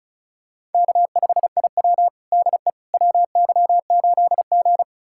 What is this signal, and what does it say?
Morse code, heard as a keyed tone, about 36 wpm, 700 Hz. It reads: K5IW DE WY8G
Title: USA Callsign DE Callsign - 4 Characters - 35wpm